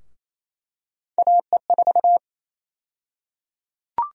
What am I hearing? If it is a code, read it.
Morse code, heard as a keyed tone, about 28 wpm, 700 Hz. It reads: AE4
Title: USA Call Sign Prefixes - 28wpm